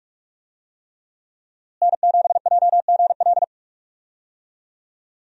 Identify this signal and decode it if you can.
Morse code, heard as a keyed tone, about 44 wpm, 700 Hz. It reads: N7JGL
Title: USA Call Signs - 5 Characters - Repeated 3x - 45wpm